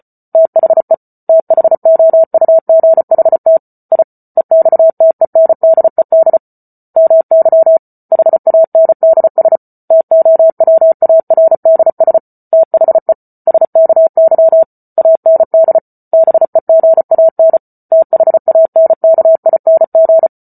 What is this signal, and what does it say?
Morse code, heard as a keyed tone, about 34 wpm, 650 Hz. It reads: THE THOUGHT I EXTENDED MY HANDS TOWARDS THE SKY AND BEGAN THANKING